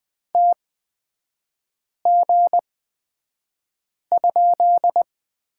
Morse code, heard as a keyed tone, about 20 wpm, spaced as for 5 wpm, 700 Hz. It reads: TG?
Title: 236 Lesson 38e - Letter q - Three Characters 20wpm